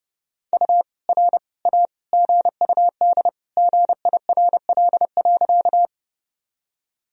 Morse code, heard as a keyed tone, about 30 wpm, 700 Hz. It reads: U R A GUD GIRL.
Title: Sentences from Top 400 Words and Abbreviations 30wpm